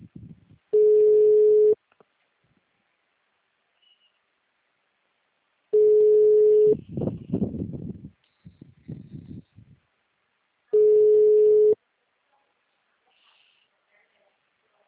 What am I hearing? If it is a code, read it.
Ringback tone (Europe, 425 Hz).
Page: Sao J. dos Campos - SP